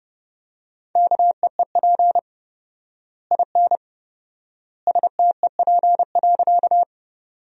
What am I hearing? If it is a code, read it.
Morse code, heard as a keyed tone, about 30 wpm, 700 Hz. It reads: KEEP IN STEP.
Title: 4x Word-Spacing - Sentences from Top 400 Words - 30wpm